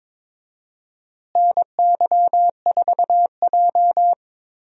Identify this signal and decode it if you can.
Morse code, heard as a keyed tone, about 23 wpm, 700 Hz. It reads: NY4J
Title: USA Call Signs - 4 Characters - Repeated 3x - 22wpm